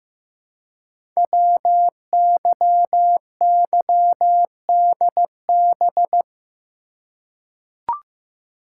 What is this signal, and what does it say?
Morse code, heard as a keyed tone, about 15 wpm, 700 Hz. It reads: WYYDB
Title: Code Groups - 5 Letters - 15wpm